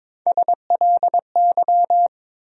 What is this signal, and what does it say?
Morse code, heard as a keyed tone, about 22 wpm, 700 Hz. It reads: SLY
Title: USA Call Sign Postfixes - 3 Letters 22wpm